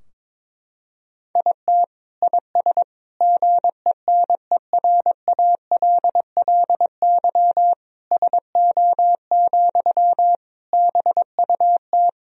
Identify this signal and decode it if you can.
Morse code, heard as a keyed tone, about 22 wpm, 700 Hz. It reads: IT IS GENERALLY SO, BUT
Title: Sentences from Top 1000 Words - Review Part 4 - 22wpm